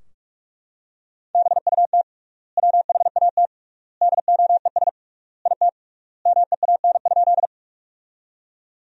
Morse code, heard as a keyed tone, about 43 wpm, 700 Hz. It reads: BUT WHAT DOES IT MEAN?
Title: 3x Word-Spacing - Sentences from Top 200 Words - 45wpm